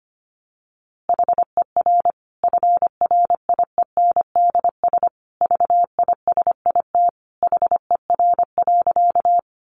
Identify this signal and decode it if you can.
Morse code, heard as a keyed tone, about 25 wpm, 700 Hz. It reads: HER FRIENDS VISIT HER.
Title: Easy Phrases 25wpm